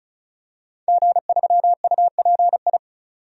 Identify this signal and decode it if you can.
Morse code, heard as a keyed tone, about 36 wpm, 700 Hz. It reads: G3UPI
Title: DXCC-100 Call Signs - 5 Characters - 35wpm